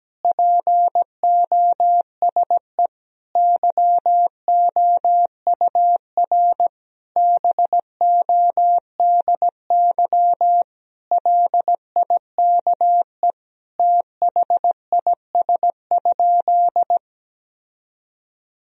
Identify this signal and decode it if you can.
Morse code, heard as a keyed tone, about 17 wpm, 700 Hz. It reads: POSE YOUR BODY LIKE THIS?
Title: Sentences from Top 1000 Words - Review Part 2 - 17wpm